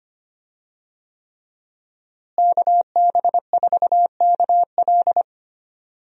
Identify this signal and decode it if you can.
Morse code, heard as a keyed tone, about 25 wpm, 700 Hz. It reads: KB4KL
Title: USA Call Signs - 5 Characters - Repeated 3x - 25wpm